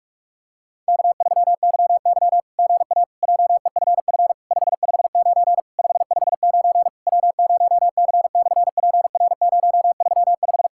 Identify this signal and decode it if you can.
Morse code, heard as a keyed tone, about 46 wpm, 690 Hz. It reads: K3YY GA JEFF 559 559 W0C/PR035
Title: SOTA QSO Practice 45wpm